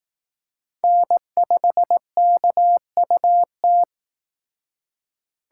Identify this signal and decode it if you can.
Morse code, heard as a keyed tone, about 18 wpm, 700 Hz. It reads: N5KUT
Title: Speed Racing - USA 1x3 Call Signs 15wpm